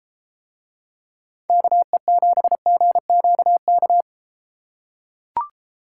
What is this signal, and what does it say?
Morse code, heard as a keyed tone, about 31 wpm, 700 Hz. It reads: KE7GQK